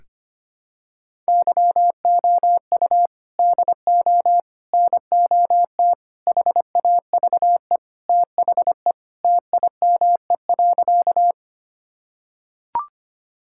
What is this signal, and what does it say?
Morse code, heard as a keyed tone, about 25 wpm, 700 Hz. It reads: YOU DO NOT HAVE THE TIME.
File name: practice / 25wpm / Sentences with 4-Word Phrases - 25wpm